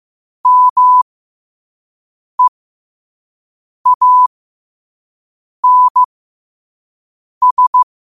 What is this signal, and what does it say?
Morse code, heard as a keyed tone, about 15 wpm, spaced as for 5 wpm, 1000 Hz. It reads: MEANS
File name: 5wpm